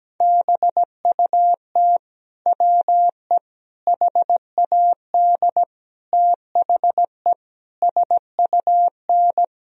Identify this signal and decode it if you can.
Morse code, heard as a keyed tone, about 17 wpm, 700 Hz. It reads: BUT WE HAD THE SUN
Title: Sentences from Top 1000 Words - Review Part 3 - 17wpm